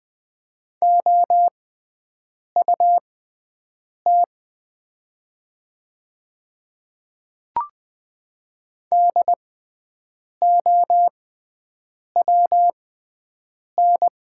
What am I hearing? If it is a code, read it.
Morse code, heard as a keyed tone, about 20 wpm, spaced as for 7 wpm, 700 Hz. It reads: OUT DOWN
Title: 6x Character Spacing - Top 100 Words - 20wpm